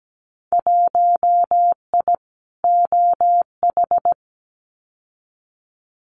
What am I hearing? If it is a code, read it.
Morse code, heard as a keyed tone, about 17 wpm, 700 Hz. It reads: 1I OH
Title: Winter Field Day Exchanges 17wpm